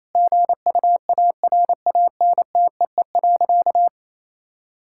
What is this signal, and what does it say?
Morse code, heard as a keyed tone, about 28 wpm, 700 Hz. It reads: GUARANTEE.
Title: Sentences from Top 1600 Words 28wpm